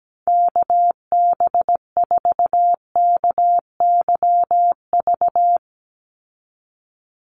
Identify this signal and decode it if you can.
Morse code, heard as a keyed tone, about 17 wpm, 700 Hz. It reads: KB4KYV
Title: USA Call Signs 17wpm